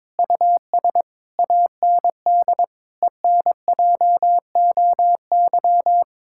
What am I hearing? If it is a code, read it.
Morse code, heard as a keyed tone, about 22 wpm, 700 Hz. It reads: US AND ENJOY
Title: Sentences from Top 1000 Words - Review Part 5 - 22wpm